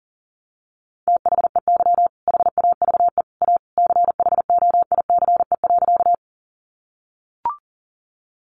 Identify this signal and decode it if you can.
Morse code, heard as a keyed tone, about 40 wpm, 700 Hz. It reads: THEY HAVE A CHOICE.
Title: Sentences from Top 900 Words 40wpm